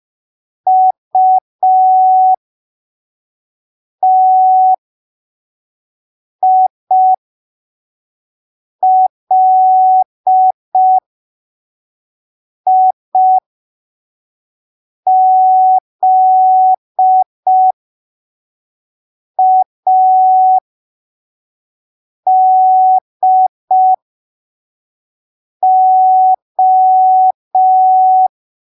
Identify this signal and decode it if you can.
Morse code, heard as a keyed tone, about 5 wpm, spaced as for 3 wpm, 750 Hz. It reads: UTILIZADO
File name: arquivos / texto / texto 14 05ppm